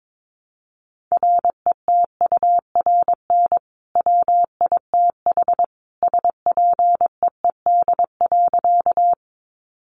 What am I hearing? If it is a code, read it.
Morse code, heard as a keyed tone, about 22 wpm, 700 Hz. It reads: RETURN WITH SPEED.
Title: Sentences from Top 800 Words 22wpm